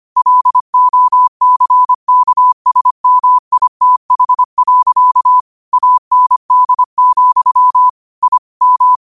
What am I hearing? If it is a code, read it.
Morse code, heard as a keyed tone, about 25 wpm, 1000 Hz. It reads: LOCKSMITH. AND, I M
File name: qotdfiles / 25wpm